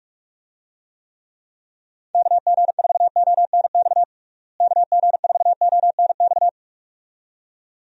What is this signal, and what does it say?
Morse code, heard as a keyed tone, about 46 wpm, 700 Hz. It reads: KG4ONX KG4ONX
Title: USA Call Signs - 6 Characters - Repeated 2x - 45wpm